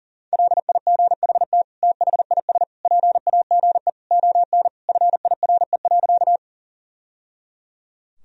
Morse code, heard as a keyed tone, about 40 wpm, 700 Hz. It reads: LIGHT THIS PAGE ON FIRE.
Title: Sentences from Top 1000 Words - Review Part 1 - 40wpm